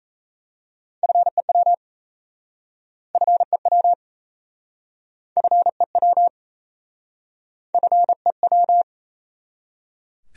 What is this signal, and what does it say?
Morse code, heard as a keyed tone, about 37 wpm, 700 Hz. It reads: FEW FEW FEW FEW